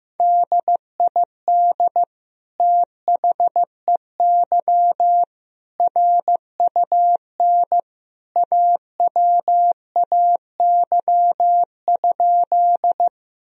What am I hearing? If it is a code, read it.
Morse code, heard as a keyed tone, about 15 wpm, 700 Hz. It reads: DID THEY RUN AWAY?
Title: Sentences from Top 1000 Words - Review Part 1 - 15wpm